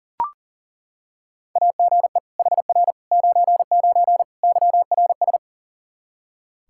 Morse code, heard as a keyed tone, about 40 wpm, 700 Hz. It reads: AGE HR 99 YRS
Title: Practice QSO - Exchange 3 - More Info - 40wpm